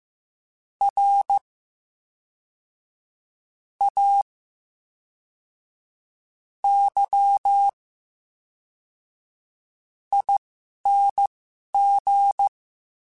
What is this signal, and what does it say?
Morse code, heard as a keyed tone, about 15 wpm, spaced as for 5 wpm, 800 Hz. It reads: RAYING
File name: Special INGs / Special INGs-mp3s / VerySlowToFaster Ing @10